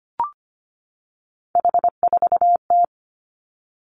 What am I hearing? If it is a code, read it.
Morse code, heard as a keyed tone, about 25 wpm, 700 Hz. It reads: H4T
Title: Three Challenging Letter-Number Combinations 25wpm